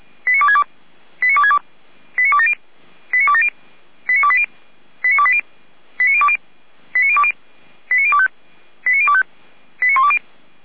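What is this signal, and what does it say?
Five-tone selective call (ZVEI), heard as quick runs of five tones, tones of about 80 ms. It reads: 89262 89262 89270 89270 89280 89280 89020 89020 89025 89025 89120
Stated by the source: Five-tone ZVEI selective call received by SDR.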